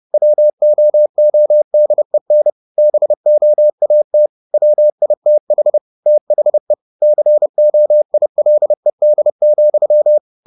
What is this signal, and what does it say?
Morse code, heard as a keyed tone, about 30 wpm, 600 Hz. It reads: WOODEN BOAT WITH THE COILED,
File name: the old man and the sea / the old man and the sea 1 30wpm